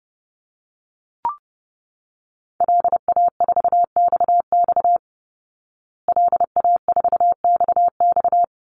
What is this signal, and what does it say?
Morse code, heard as a keyed tone, about 30 wpm, 700 Hz. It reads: LA4XX LA4XX
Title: Call Signs Repeated 3x - 30wpm